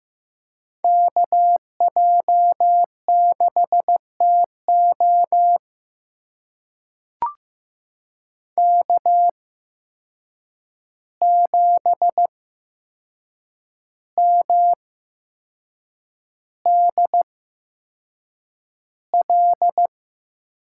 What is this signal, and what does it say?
Morse code, heard as a keyed tone, about 15 wpm, spaced as for 6 wpm, 700 Hz. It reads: KJ6TO K7MDL
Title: USA Call Signs - 5 Characters - 8x Character Spacing - 15wpm